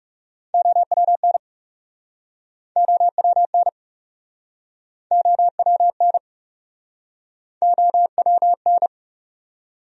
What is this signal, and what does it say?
Morse code, heard as a keyed tone, about 41 wpm, 700 Hz. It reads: OWN OWN OWN OWN